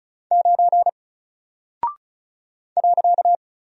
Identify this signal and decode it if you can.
Morse code, heard as a keyed tone, about 36 wpm, 700 Hz. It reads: 9 .
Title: Single Character (Letter:Number:Punctuation:Prosign) Alt 35wpm